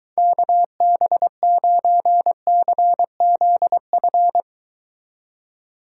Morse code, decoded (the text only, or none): KB9CZF